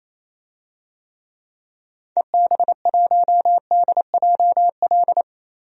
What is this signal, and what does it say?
Morse code, heard as a keyed tone, about 27 wpm, 700 Hz. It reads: EB1DJL
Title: DXCC-100 Call Signs - 6 Characters - Repeated 2x - 28wpm